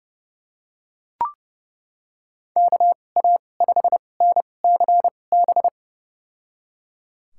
Morse code, heard as a keyed tone, about 30 wpm, spaced as for 22 wpm, 700 Hz. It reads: KA5NCB
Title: USA Call Signs - 2x Character Spacing - 30wpm